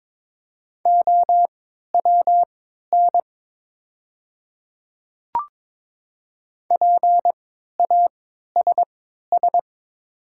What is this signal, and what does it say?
Morse code, heard as a keyed tone, about 22 wpm, spaced as for 13 wpm, 690 Hz. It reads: OWN PASS
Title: 3x Character Spacing - Top 500 Words - 22wpm